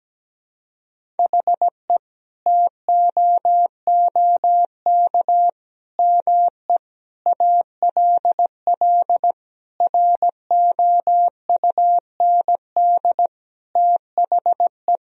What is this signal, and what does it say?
Morse code, heard as a keyed tone, about 17 wpm, 700 Hz. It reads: HE TOOK ME ALL ROUND THE
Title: Sentences from Top 1000 Words - Review Part 1 - 17wpm